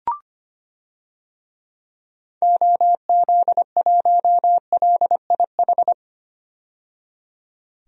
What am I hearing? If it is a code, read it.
Morse code, heard as a keyed tone, about 25 wpm, 700 Hz. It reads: OZ1LIH
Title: DXCC-100 Call Signs - 6 Characters - 25wpm